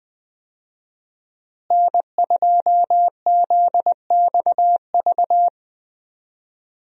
Morse code, heard as a keyed tone, about 20 wpm, 700 Hz. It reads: N2ZXV
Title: USA Call Signs - 5 Characters - Repeated 3x - 20wpm